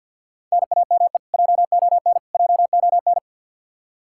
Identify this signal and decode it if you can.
Morse code, heard as a keyed tone, about 50 wpm, 700 Hz. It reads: NAME JON JON
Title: K1USN SST QSO Practice 50wpm